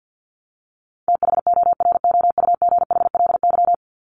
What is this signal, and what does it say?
Morse code, heard as a keyed tone, about 50 wpm, 700 Hz. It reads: THOROUGHLY
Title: LY Suffix - Top 100 Words with Base Words 50wpm